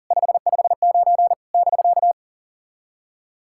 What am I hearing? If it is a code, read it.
Morse code, heard as a keyed tone, about 40 wpm, 700 Hz. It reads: 559 <BK>
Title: SOTA QSO Practice 40wpm